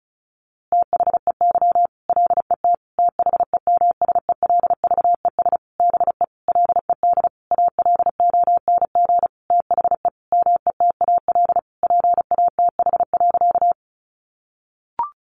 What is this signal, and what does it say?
Morse code, heard as a keyed tone, about 34 wpm, 700 Hz. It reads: THEY LET THEMSELVES BE LED ALONG THE METAL PATH.
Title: A Sound Of Thunder 35wpm